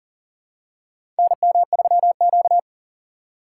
Morse code, heard as a keyed tone, about 37 wpm, 700 Hz. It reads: NM3Q
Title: USA Call Signs - 4 Characters - 2x Character Spacing - 40wpm